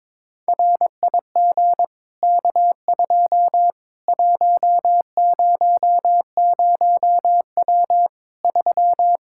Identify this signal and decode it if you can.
Morse code, heard as a keyed tone, about 22 wpm, 700 Hz. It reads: RIG K2 100W 3
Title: K1USN SST QSO Practice 22wpm